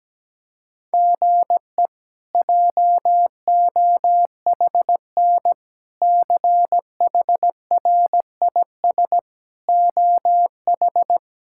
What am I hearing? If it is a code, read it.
Morse code, heard as a keyed tone, about 17 wpm, 700 Hz. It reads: GE JOHN CHRIS OH
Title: K1USN SST QSO Practice - 17wpm